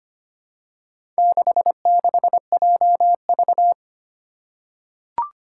Morse code, heard as a keyed tone, about 25 wpm, 700 Hz. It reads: 66JV
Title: Four Challenging Letter-Number Combinations 25wpm